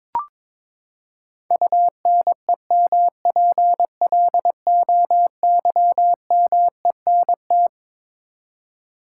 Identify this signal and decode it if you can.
Morse code, heard as a keyed tone, about 22 wpm, 700 Hz. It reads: UNEMPLOYMENT
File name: Twelve Letter Words - 22wpm